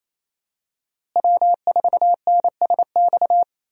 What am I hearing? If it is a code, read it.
Morse code, heard as a keyed tone, about 28 wpm, 700 Hz. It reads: W4NSX
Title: USA Call Signs - 5 Characters - Repeated 3x - 28wpm